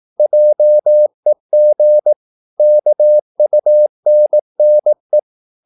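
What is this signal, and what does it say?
Morse code, heard as a keyed tone, about 18 wpm, 600 Hz. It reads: JEG KUNNE
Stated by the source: Slow (QRS) Morse practice at 18 wpm, CW med Gnister.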